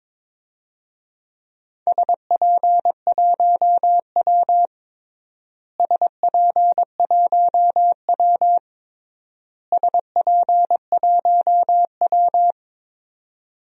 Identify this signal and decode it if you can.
Morse code, heard as a keyed tone, about 22 wpm, 700 Hz. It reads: SP1W SP1W SP1W
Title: DXCC-100 Call Signs - 4 Characters - Repeated 3x - 22wpm